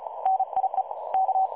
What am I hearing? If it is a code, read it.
Morse code, heard as a keyed tone, about 36 wpm, 760 Hz. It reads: NIE K
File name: dlazaawansowanych / nagraniazpasma / websdr recording 3565.0kHz